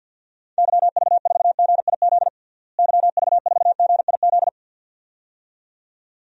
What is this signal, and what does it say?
Morse code, heard as a keyed tone, about 47 wpm, 700 Hz. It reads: YV4GIZ YV4GIZ
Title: DXCC-100 Call Signs - 6 Characters - Repeated 2x - 50wpm